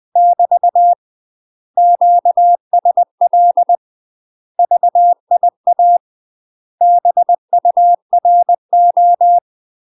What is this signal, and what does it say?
Morse code, heard as a keyed tone, about 20 wpm, 700 Hz. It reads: <BT> QSL VIA BURO
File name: UK MASTER QSOs TRAINING/20WPM